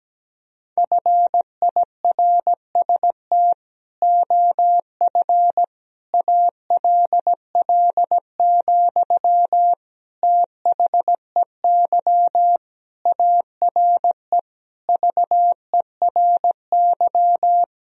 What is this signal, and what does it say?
Morse code, heard as a keyed tone, about 17 wpm, 700 Hz. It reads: FIRST OF ALL, THEY ARE VERY
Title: Sentences from Top 1000 Words - Review Part 4 - 17wpm